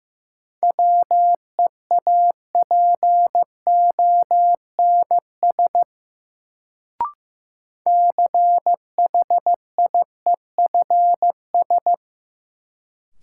Morse code, heard as a keyed tone, about 15 wpm, 700 Hz. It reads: WEAPONS CHIEFS
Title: Sets of 5 Words Encore - New Word Drill - 15wpm